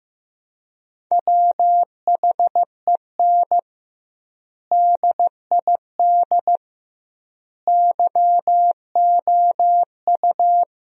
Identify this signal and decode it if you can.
Morse code, heard as a keyed tone, about 15 wpm, 700 Hz. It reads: WHEN DID YOU
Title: 2x Word-Spacing - Sentences from Top 300 Words - 15wpm